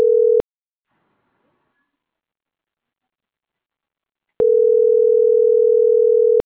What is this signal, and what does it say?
Ringback tone (North America), 440 + 480 Hz.